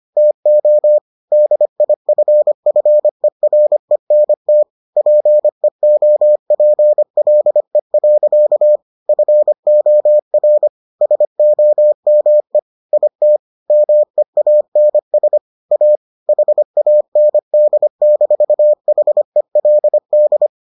Morse code, heard as a keyed tone, about 25 wpm, 600 Hz. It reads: TO DIFFERENT PEOPLE. FOR SOME IT MEANS A HAND-HELD